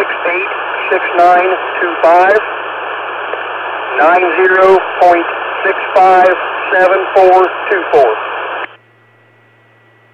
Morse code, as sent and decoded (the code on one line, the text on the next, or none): none